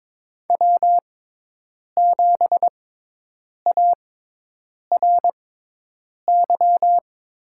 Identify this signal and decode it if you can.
Morse code, heard as a keyed tone, about 22 wpm, spaced as for 8 wpm, 700 Hz. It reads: W7ARY
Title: USA Call Signs - 5 Characters - 6x Character Spacing - 22wpm